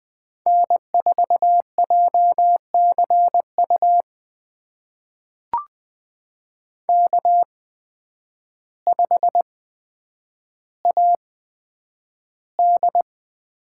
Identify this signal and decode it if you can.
Morse code, heard as a keyed tone, about 20 wpm, spaced as for 9 wpm, 700 Hz. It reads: N4JCU K5AD
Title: USA Call Signs - 5 Characters - 8x Character Spacing - 20wpm